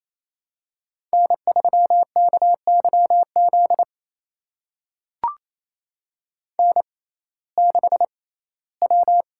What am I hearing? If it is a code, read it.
Morse code, heard as a keyed tone, about 28 wpm, spaced as for 17 wpm, 700 Hz. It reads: N3KYZ N6W
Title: USA Call Signs - 5 Characters - 6x Character Spacing - 28wpm